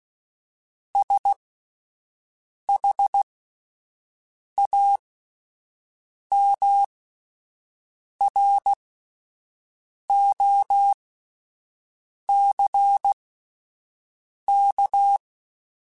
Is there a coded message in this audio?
Yes: Morse code from 0.9 to 15.2 s, heard as a keyed tone, about 16 wpm, spaced as for 5 wpm, 800 Hz. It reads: SHAMROCK